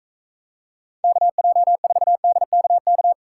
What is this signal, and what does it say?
Morse code, heard as a keyed tone, about 45 wpm, 700 Hz. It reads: KJ4DKK